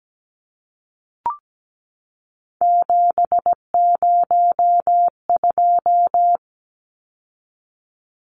Morse code, heard as a keyed tone, about 17 wpm, 700 Hz. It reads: 702
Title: Numbers - 3-Digit 17wpm